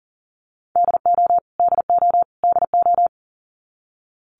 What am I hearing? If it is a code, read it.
Morse code, heard as a keyed tone, about 40 wpm, 700 Hz. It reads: DO DO DO
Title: Top 200 Words Repeated 40wpm